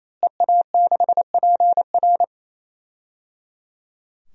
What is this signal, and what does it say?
Morse code, heard as a keyed tone, about 28 wpm, 700 Hz. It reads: EA6PR
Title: DXCC-100 Call Signs - 5 Characters - Repeated 3x - 28wpm